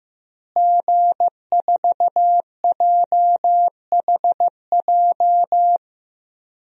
Morse code, heard as a keyed tone, about 15 wpm, 700 Hz. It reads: G4JHJ
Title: DXCC-100 Call Signs - 5 Characters - Repeated 3x - 15wpm